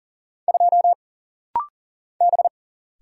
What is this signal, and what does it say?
Morse code, heard as a keyed tone, about 41 wpm, 700 Hz. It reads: 2 B
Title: Single Letter-Number Alt - Rapid-Fire 40wpm